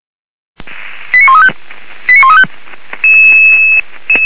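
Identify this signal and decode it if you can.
Five-tone selective call (ZVEI), heard as quick runs of five tones, tones of about 70 ms. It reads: 89135 89135
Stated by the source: Five-tone ZVEI selective call received on an SDR.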